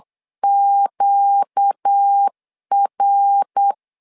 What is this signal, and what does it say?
Morse code, heard as a keyed tone, about 8 wpm, 800 Hz. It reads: QR
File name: 4 Q 8wpm